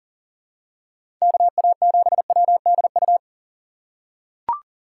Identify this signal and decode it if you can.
Morse code, heard as a keyed tone, about 37 wpm, 700 Hz. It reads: KA7WDU
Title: USA Call Signs - 6x Character Spacing - 40wpm